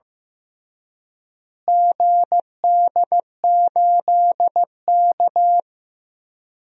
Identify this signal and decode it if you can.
Morse code, heard as a keyed tone, about 15 wpm, 700 Hz. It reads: GD8K
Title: DXCC-100 Call Signs - 4 Characters - Repeated 3x - 15wpm